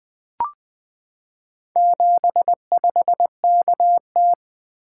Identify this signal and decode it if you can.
Morse code, heard as a keyed tone, about 21 wpm, 700 Hz. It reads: 75KT
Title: Four Letter-Number Combinations - 20wpm